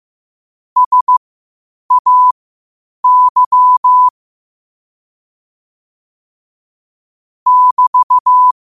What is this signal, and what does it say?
Morse code, heard as a keyed tone, about 15 wpm, spaced as for 8 wpm, 1000 Hz. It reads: SAY <BT>
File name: qotdfiles / 8wpm